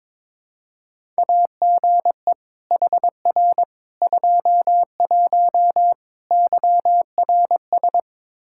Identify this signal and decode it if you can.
Morse code, heard as a keyed tone, about 22 wpm, 700 Hz. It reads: AGE HR 21 YRS
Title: Practice QSO - Exchange 3 - More Info - 22wpm